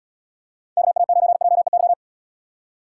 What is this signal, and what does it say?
Morse code, heard as a keyed tone, about 69 wpm, 700 Hz. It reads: KA9JEY